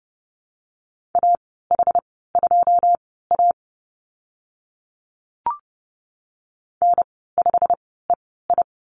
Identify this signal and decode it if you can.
Morse code, heard as a keyed tone, about 30 wpm, spaced as for 17 wpm, 700 Hz. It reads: AH2A N5EI
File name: Lesson 15g - Number 5 - Call Signs 30wpm